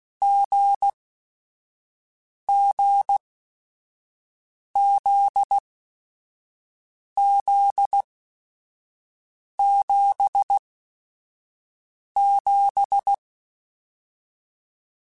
Morse code, heard as a keyed tone, about 16 wpm, spaced as for 5 wpm, 800 Hz. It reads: GGZZ77